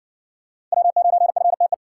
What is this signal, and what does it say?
Morse code, heard as a keyed tone, about 60 wpm, 700 Hz. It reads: W9FTE